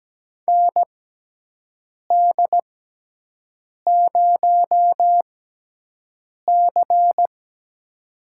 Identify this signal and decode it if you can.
Morse code, heard as a keyed tone, about 17 wpm, spaced as for 6 wpm, 700 Hz. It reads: ND0C
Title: USA Call Signs - 4 Characters - 6x Character Spacing - 17wpm